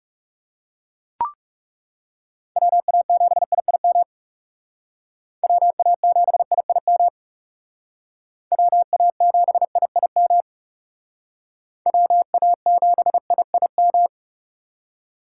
Morse code, heard as a keyed tone, about 36 wpm, 700 Hz. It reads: WA7IIM WA7IIM WA7IIM WA7IIM
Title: Speed Racing - USA 2x3 Call Signs 30wpm